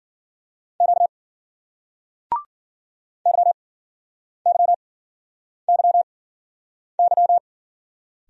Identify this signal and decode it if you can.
Morse code, heard as a keyed tone, about 52 wpm, spaced as for 9 wpm, 700 Hz. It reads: Q YYYY